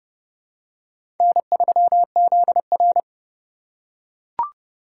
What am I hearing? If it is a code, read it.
Morse code, heard as a keyed tone, about 27 wpm, 700 Hz. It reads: N3ZR